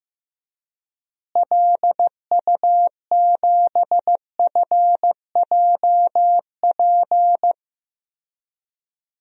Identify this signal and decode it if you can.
Morse code, heard as a keyed tone, about 15 wpm, 700 Hz. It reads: LU7FJP
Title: DXCC-100 Call Signs - 6 Characters - Repeated 3x - 15wpm